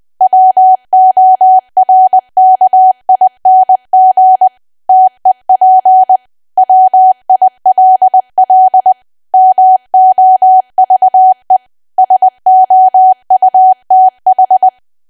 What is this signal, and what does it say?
Morse code, heard as a keyed tone, about 20 wpm, 750 Hz. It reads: WORKING TEP WILL MOVE SOUTH
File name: cwa / W1AW-Level3 / 20WPM